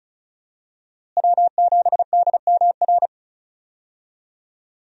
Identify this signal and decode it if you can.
Morse code, heard as a keyed tone, about 36 wpm, 700 Hz. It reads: W7DMR
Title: USA Call Signs - 5 Characters - Repeated 3x - 35wpm